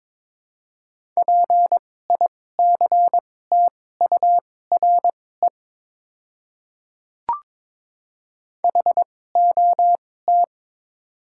Morse code, heard as a keyed tone, about 21 wpm, spaced as for 16 wpm, 700 Hz. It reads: PICTURE HOT
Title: 2x Character Spacing - Top 200 Words - 22wpm